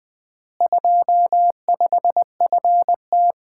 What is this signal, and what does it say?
Morse code, heard as a keyed tone, about 20 wpm, 700 Hz. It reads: 25FT
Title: Common QSO Phrases - 4x Word-Spacing - 20wpm